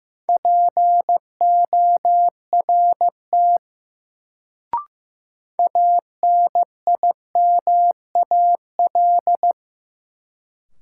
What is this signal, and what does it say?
Morse code, heard as a keyed tone, about 15 wpm, 700 Hz. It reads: PORT ANIMAL
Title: Sentences from Top 1000 Words - Review Part 1 - New Word Drill - 15wpm